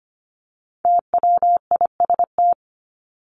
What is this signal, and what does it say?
Morse code, heard as a keyed tone, about 25 wpm, 700 Hz. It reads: TWIST
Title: Top 1751-2000 Words 25wpm